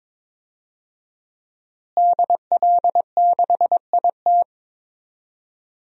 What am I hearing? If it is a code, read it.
Morse code, heard as a keyed tone, about 22 wpm, 700 Hz. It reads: DL6IT
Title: DXCC-100 Call Signs - 5 Characters - Repeated 2x - 22wpm